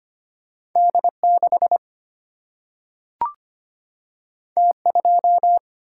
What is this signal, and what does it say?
Morse code, heard as a keyed tone, about 25 wpm, 700 Hz. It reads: D6 T2
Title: Single-Letter Single-Number 25wpm